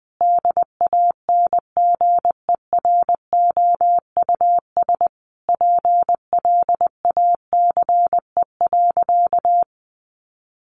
Morse code, heard as a keyed tone, about 20 wpm, 700 Hz. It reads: DANGEROUS PLACE.
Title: Sentences from Top 1200 Words 20wpm